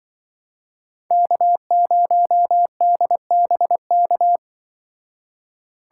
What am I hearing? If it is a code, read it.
Morse code, heard as a keyed tone, about 23 wpm, 700 Hz. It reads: K0DBK
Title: Speed Racing - USA 1x3 Call Signs 20wpm